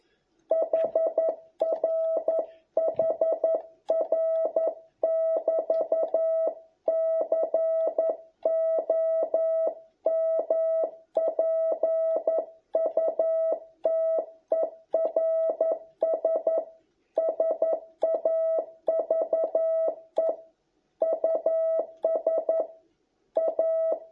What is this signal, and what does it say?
Morse code, heard as a keyed tone, about 11 wpm, 650 Hz. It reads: HR HR <BT> COMPUTERS SAVE US A